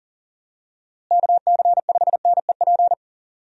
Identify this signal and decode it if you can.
Morse code, heard as a keyed tone, about 40 wpm, 700 Hz. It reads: KC5NEP
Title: USA Call Signs - 6 Characters - Repeated 2x - 40wpm